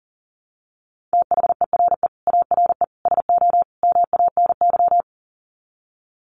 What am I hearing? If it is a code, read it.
Morse code, heard as a keyed tone, about 37 wpm, 700 Hz. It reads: THERE ARE SO MANY
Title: Sets of 4 Words 40wpm